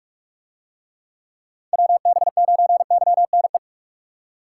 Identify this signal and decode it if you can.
Morse code, heard as a keyed tone, about 44 wpm, 700 Hz. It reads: WB9YNE